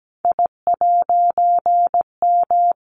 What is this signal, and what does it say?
Morse code, heard as a keyed tone, about 17 wpm, 700 Hz. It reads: I'M
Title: A Sound Of Thunder 17wpm